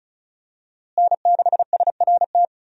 Morse code, heard as a keyed tone, about 35 wpm, 700 Hz. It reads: N6SRT
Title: USA Call Signs - 5 Characters - Repeated 2x - 35wpm